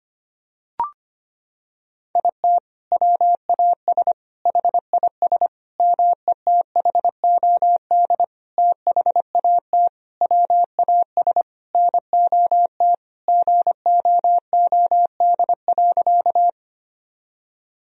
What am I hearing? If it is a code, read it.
Morse code, heard as a keyed tone, about 25 wpm, 700 Hz. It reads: IT WAS HIS METHOD THAT WAS NOT GOOD.
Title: Sentences from Top 1000 Words - Review Part 3 - 25wpm